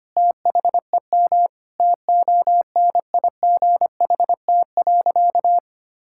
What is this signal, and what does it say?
Morse code, heard as a keyed tone, about 25 wpm, 700 Hz. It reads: THEM TONIGHT.
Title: Sentences from Top 1000 Words - Review Part 6 - 25wpm